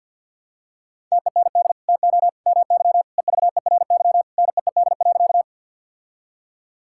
Morse code, heard as a keyed tone, about 50 wpm, 700 Hz. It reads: TEND TO MY EVERY NEED.
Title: Sentences from Top 1000 Words - Review Part 5 - 50wpm